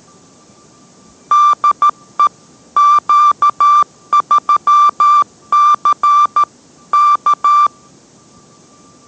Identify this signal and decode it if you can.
Morse code, heard as a keyed tone, about 13 wpm, 1180 Hz. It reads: DE Q3C K